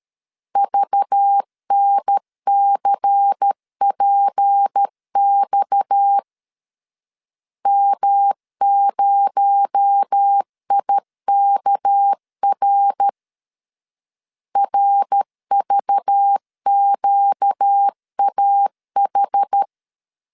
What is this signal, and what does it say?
Morse code, heard as a keyed tone, about 13 wpm, 800 Hz. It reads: VNCPX M0IKR RVQAH
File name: LEZIONI / 6 PKX0 / 6 PKX0 12wpm 2231